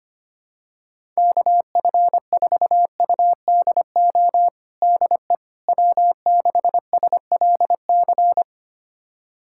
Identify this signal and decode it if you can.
Morse code, heard as a keyed tone, about 25 wpm, 700 Hz. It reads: KF4UDO DE W6SLC
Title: USA Callsign DE Callsign 25wpm